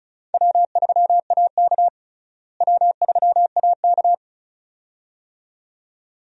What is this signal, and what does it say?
Morse code, heard as a keyed tone, about 34 wpm, 700 Hz. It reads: W3AK W3AK
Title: USA Call Signs - 4 Characters - Repeated 2x - 35wpm